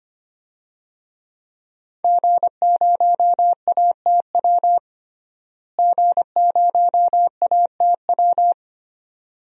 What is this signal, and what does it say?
Morse code, heard as a keyed tone, about 25 wpm, 700 Hz. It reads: G0ATW G0ATW
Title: DXCC-100 Call Signs - 5 Characters - Repeated 2x - 25wpm